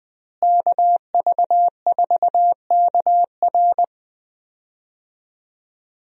Morse code, heard as a keyed tone, about 20 wpm, 700 Hz. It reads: KV4KR
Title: USA Call Signs - 5 Characters - Repeated 3x - 20wpm